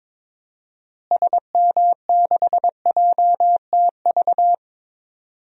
Morse code, heard as a keyed tone, about 22 wpm, 700 Hz. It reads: SM6JTV
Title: DXCC-100 Call Signs - 6 Characters - Repeated 2x - 22wpm